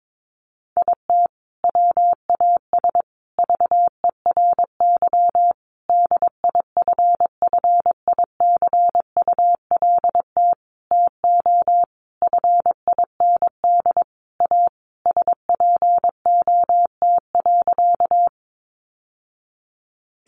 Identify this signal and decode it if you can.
Morse code, heard as a keyed tone, about 22 wpm, 700 Hz. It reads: IT WAS VERY DIFFICULT TO FIND A SPOT.
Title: Sentences from Top 1100 Words 22wpm